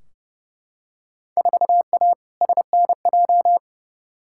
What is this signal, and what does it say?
Morse code, heard as a keyed tone, about 30 wpm, 700 Hz. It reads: 4A SNJ
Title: Field Day Exchanges Encore - 30wpm